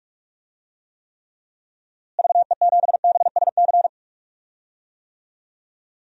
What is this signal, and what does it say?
Morse code, heard as a keyed tone, about 46 wpm, 700 Hz. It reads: VE7BSC
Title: DXCC-100 Call Signs - 6 Characters - Repeated 2x - 45wpm